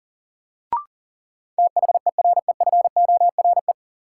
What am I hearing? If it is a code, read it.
Morse code, heard as a keyed tone, about 40 wpm, 700 Hz. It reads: THEREFORE
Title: Sentences from Top 800 Words - New Word Drill - 40wpm